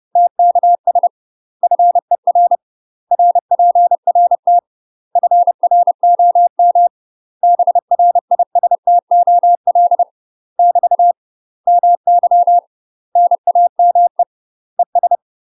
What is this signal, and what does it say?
Morse code, heard as a keyed tone, about 30 wpm, 700 Hz. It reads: TKS FER RPRT FROM BRISTOL <BT> MY NAME ES
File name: UK MASTER QSOs TRAINING / 30WPM / UK MASTER QSOs 0011